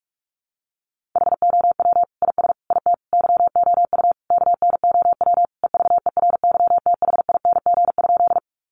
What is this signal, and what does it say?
Morse code, heard as a keyed tone, about 44 wpm, 700 Hz. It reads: HOW IS IT YOU KNOW EVERYTHING?
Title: Sentences from Top 700 Words 45wpm